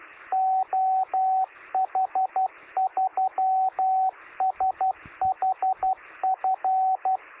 Morse code, heard as a keyed tone, about 12 wpm, 760 Hz, with radio noise. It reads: OH3SHF
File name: sm7lcb / 1959z oh3shf 9cm